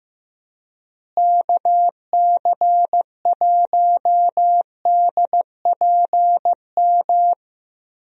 Morse code, heard as a keyed tone, about 15 wpm, 700 Hz. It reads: KC1DPM
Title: USA Call Signs - 6 Characters - Repeated 3x - 15wpm